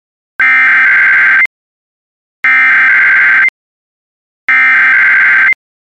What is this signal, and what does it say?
Emergency Alert System (SAME) tones (header bursts), heard as harsh buzzing bursts. It is heard from 0.4 to 5.5 s.